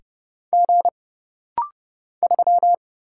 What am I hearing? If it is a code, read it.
Morse code, heard as a keyed tone, about 30 wpm, 700 Hz. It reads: G 3
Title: Single Letter-Number Alt - Rapid-Fire 30wpm